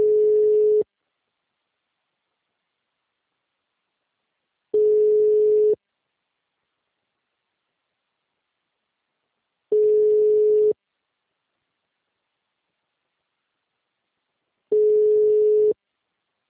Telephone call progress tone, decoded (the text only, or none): ringback tone